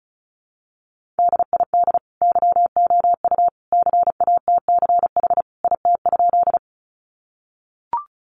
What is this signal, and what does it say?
Morse code, heard as a keyed tone, about 34 wpm, 700 Hz. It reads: DID YOU CATCH IT?
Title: Sentences from Top 1500 Words 35wpm